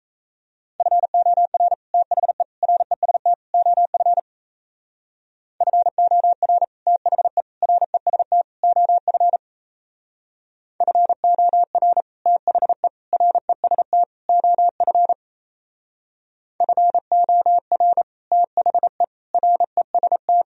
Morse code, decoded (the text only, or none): FOR THE REST OF FOR THE REST OF FOR THE REST OF FOR THE REST